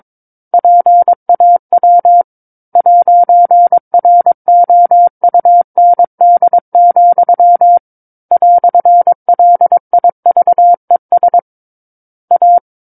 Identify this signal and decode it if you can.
Morse code, heard as a keyed tone, about 22 wpm, 700 Hz. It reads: PAW 'ROUND, "LIVES A
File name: 06-Chapter-06 22 WPM